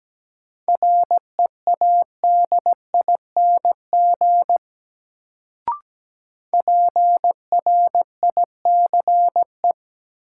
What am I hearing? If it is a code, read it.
Morse code, heard as a keyed tone, about 17 wpm, 700 Hz. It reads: READING PRICE
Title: Sentences from Top 600 Words - New Word Drill - 17wpm